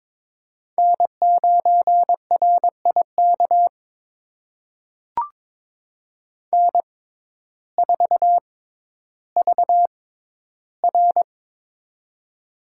Morse code, heard as a keyed tone, about 23 wpm, spaced as for 12 wpm, 700 Hz. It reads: N9RIK N4VR
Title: USA Call Signs - 6x Character Spacing - 22wpm